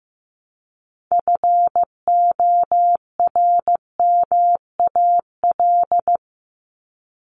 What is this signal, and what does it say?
Morse code, heard as a keyed tone, about 15 wpm, 700 Hz. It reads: FORMAL
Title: Top 1251-1500 Words 15wpm